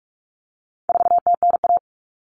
Morse code, heard as a keyed tone, about 44 wpm, 700 Hz. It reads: 4TNA
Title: Common Cut Numbers - 4-Digit 45wpm